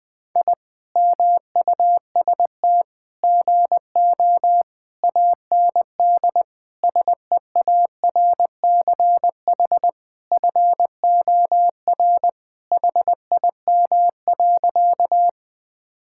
Morse code, heard as a keyed tone, about 21 wpm, 700 Hz. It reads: I MUST GO AND SEARCH FOR HIM.
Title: Sentences from Top 1000 Words - Review Part 5 - 20wpm